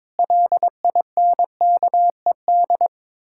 Morse code, heard as a keyed tone, about 22 wpm, 700 Hz. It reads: LINKED
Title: Pangram Challenge v2 - 22wpm